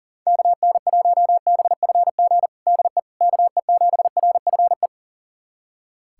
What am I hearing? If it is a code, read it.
Morse code, heard as a keyed tone, about 37 wpm, 700 Hz. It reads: KN1BFG DE KE7RFE
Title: USA Callsign DE Callsign - 6 Characters - 40wpm